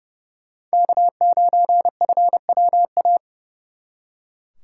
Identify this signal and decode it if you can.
Morse code, heard as a keyed tone, about 38 wpm, 700 Hz. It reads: K9FWA